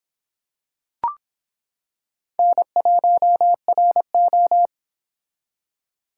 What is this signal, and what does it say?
Morse code, heard as a keyed tone, about 26 wpm, 700 Hz. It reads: N1RO